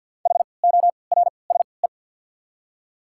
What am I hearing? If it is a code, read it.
Morse code, heard as a keyed tone, about 53 wpm, spaced as for 29 wpm, 690 Hz. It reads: HORSE
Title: 3x Character Spacing - Top 500 Words - 50wpm